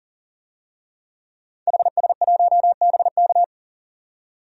Morse code, heard as a keyed tone, about 40 wpm, 700 Hz. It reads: HS1BK
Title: DXCC-100 Call Signs - 5 Characters - 40wpm